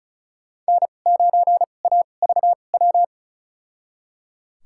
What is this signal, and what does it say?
Morse code, heard as a keyed tone, about 36 wpm, spaced as for 25 wpm, 700 Hz. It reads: N9AVW
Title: USA Call Signs - 5 Characters - 2x Character Spacing - 35wpm